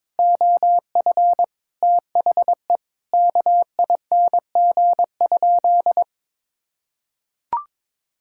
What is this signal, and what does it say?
Morse code, heard as a keyed tone, about 22 wpm, 700 Hz. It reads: OF THE KING?
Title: Sentences from Top 1000 Words - Review Part 6 - 22wpm